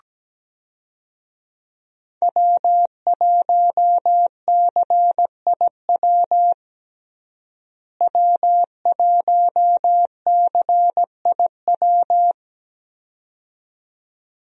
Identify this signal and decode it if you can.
Morse code, heard as a keyed tone, about 17 wpm, 700 Hz. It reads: W1CIW W1CIW
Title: USA Call Signs - 5 Characters - Repeated 2x - 17wpm